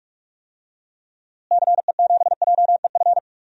Morse code, heard as a keyed tone, about 41 wpm, 700 Hz. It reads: CE7JEF
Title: DXCC-100 Call Signs - 6 Characters - Repeated 2x - 45wpm